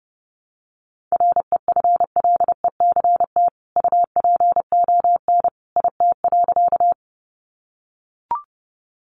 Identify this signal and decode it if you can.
Morse code, heard as a keyed tone, about 30 wpm, 700 Hz. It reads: REFLECT UPON IT.
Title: Sentences from Top 1500 Words 30wpm